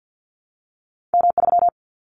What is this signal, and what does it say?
Morse code, heard as a keyed tone, about 49 wpm, 700 Hz. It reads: M3
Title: Callsign Prefixes for Top 25 DXCC - Repeated 3x - 50wpm